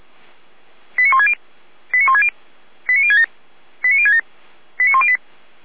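Five-tone selective call (ZVEI), heard as quick runs of five tones, tones of about 80 ms. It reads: 89160 89160 89067 89067 89108
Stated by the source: ZVEI five-tone selective call received by SDR.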